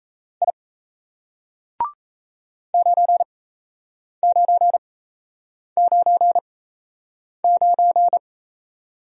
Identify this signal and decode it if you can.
Morse code, heard as a keyed tone, about 37 wpm, spaced as for 8 wpm, 700 Hz. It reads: I 9999